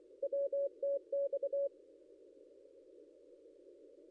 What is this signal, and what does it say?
Morse code, heard as a keyed tone, about 24 wpm, 560 Hz. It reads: WTX
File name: contests / ss-cw-k5tr-op-n5rz -0800